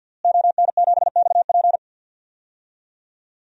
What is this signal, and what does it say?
Morse code, heard as a keyed tone, about 47 wpm, 700 Hz. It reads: ON7XP
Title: DXCC-100 Call Signs - 5 Characters - Repeated 2x - 50wpm